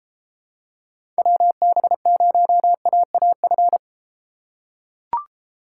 Morse code, heard as a keyed tone, about 33 wpm, 700 Hz. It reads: WB0AAF